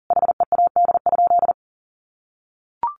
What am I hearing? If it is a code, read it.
Morse code, heard as a keyed tone, about 40 wpm, 700 Hz. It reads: HEAD?
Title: Sentences from Top 600 Words 40wpm